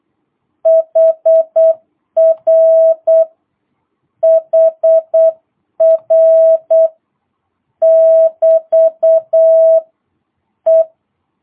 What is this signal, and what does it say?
Morse code, heard as a keyed tone, about 8 wpm, 660 Hz. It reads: HR HR <BT> E